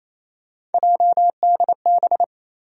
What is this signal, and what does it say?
Morse code, heard as a keyed tone, about 28 wpm, 700 Hz. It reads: JDB
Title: USA Call Sign Postfixes - 3 Letters 28wpm